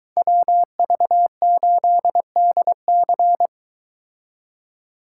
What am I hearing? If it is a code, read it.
Morse code, heard as a keyed tone, about 23 wpm, 700 Hz. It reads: WV8DC